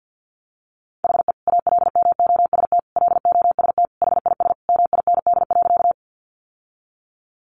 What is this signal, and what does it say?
Morse code, heard as a keyed tone, about 50 wpm, 700 Hz. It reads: HE ALMOST LOST HIS MIND.
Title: Sentences from Top 1400 Words 50wpm